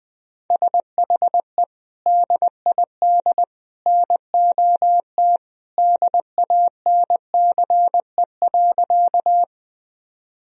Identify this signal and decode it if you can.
Morse code, heard as a keyed tone, about 20 wpm, 700 Hz. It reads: SHE DID NOT DANCE.
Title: Sentences from Top 1000 Words - Review Part 4 - 20wpm